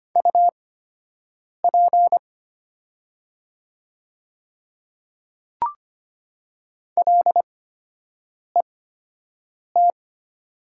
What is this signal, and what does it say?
Morse code, heard as a keyed tone, about 25 wpm, spaced as for 7 wpm, 690 Hz. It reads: UP LET
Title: 8x Character Spacing - Top 300 Words - 25wpm